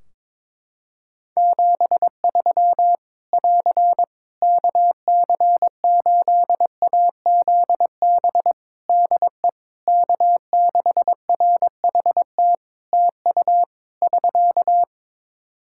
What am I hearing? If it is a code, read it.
Morse code, heard as a keyed tone, about 22 wpm, 700 Hz. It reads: 73 <AR> KC8AZB DE K6RHT TU <SK>
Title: Practice QSO - Exchange 4 - Ending - 22wpm